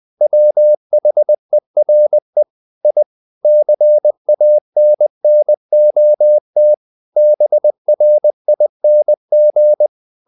Morse code, heard as a keyed tone, about 20 wpm, 600 Hz. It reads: WHERE I CANNOT BRING
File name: the old man and the sea part 2 20wpm